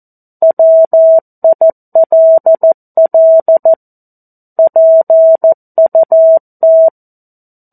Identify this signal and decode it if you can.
Morse code, heard as a keyed tone, about 14 wpm, 650 Hz. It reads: WILL PUT
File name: 02-Chapter-02 14 WPM